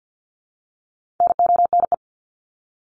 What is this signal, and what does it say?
Morse code, heard as a keyed tone, about 48 wpm, 700 Hz. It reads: NONE
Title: Top 1501-1750 Words 50wpm